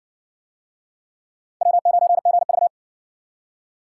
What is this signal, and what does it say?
Morse code, heard as a keyed tone, about 56 wpm, 700 Hz. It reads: W9GV